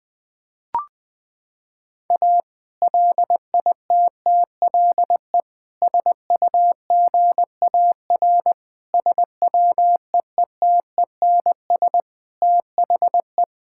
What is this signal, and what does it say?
Morse code, heard as a keyed tone, about 20 wpm, 700 Hz. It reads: A LITTLE SUGAR SWEETENS THE
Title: Sentences with 2-Word Phrases - 20wpm